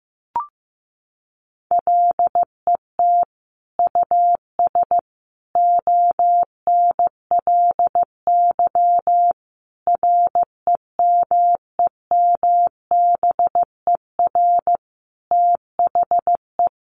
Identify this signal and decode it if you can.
Morse code, heard as a keyed tone, about 15 wpm, 700 Hz. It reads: LET US ONLY REMEMBER THE
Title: Sentences from Top 800 Words 15wpm